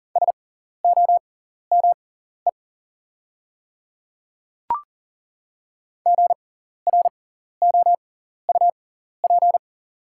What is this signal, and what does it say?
Morse code, heard as a keyed tone, about 40 wpm, spaced as for 14 wpm, 690 Hz. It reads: SOME GROUP
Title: 6x Character Spacing - Top 500 Words - 40wpm